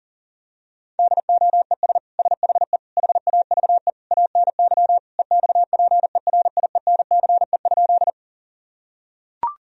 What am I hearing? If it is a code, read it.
Morse code, heard as a keyed tone, about 40 wpm, 700 Hz. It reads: DOES SHE HAVE ANY EXPERIENCE?
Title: Sentences from Top 1000 Words - Review Part 3 - 40wpm